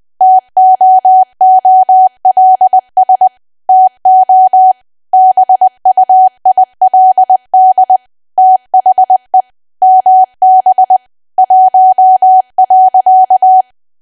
Morse code, heard as a keyed tone, about 20 wpm, 750 Hz. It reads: TOOLS TO BUILD THE MB 1.
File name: cwa / W1AW-Level3 / 20WPM